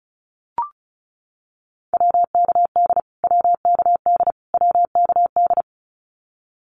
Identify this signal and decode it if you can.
Morse code, heard as a keyed tone, about 34 wpm, 700 Hz. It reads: WKD WKD WKD
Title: Abbreviations Repeated 3x - 35wpm